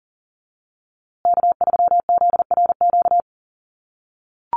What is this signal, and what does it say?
Morse code, heard as a keyed tone, about 40 wpm, 700 Hz. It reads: K3ZRQ
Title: USA Call Signs Repeated 3x 40wpm